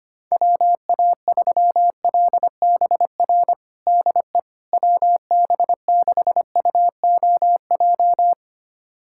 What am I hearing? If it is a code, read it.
Morse code, heard as a keyed tone, about 25 wpm, 700 Hz. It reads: WA3LBR DE WB6UOJ
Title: USA Callsign DE Callsign - 6 Characters - 25wpm